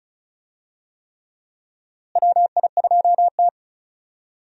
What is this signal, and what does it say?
Morse code, heard as a keyed tone, about 36 wpm, 700 Hz. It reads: WI2T
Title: USA Call Signs - 4 Characters - Repeated 3x - 35wpm